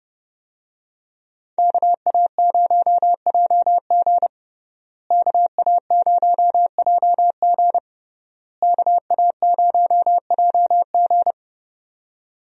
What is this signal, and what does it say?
Morse code, heard as a keyed tone, about 31 wpm, 700 Hz. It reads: KA0JG KA0JG KA0JG
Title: USA Call Signs - 5 Characters - Repeated 3x - 30wpm